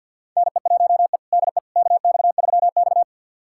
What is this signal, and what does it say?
Morse code, heard as a keyed tone, about 53 wpm, 700 Hz. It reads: NE1E DE KX3X
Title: USA Callsign DE Callsign - 4 Characters - 50wpm